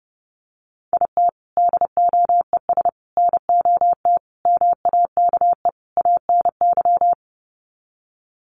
Morse code, heard as a keyed tone, about 30 wpm, 700 Hz. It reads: IT DOES NOT MAKE ANY
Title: Sets of 5 Words Encore2 30wpm